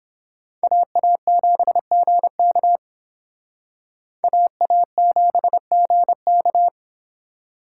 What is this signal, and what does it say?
Morse code, heard as a keyed tone, about 26 wpm, 700 Hz. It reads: AA7GK AA7GK
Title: Speed Racing - USA 2x2 Call Signs 22wpm